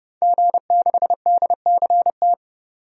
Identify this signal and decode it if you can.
Morse code, heard as a keyed tone, about 30 wpm, 700 Hz. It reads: G6DCT
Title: DXCC-100 Call Signs - 5 Characters - 30wpm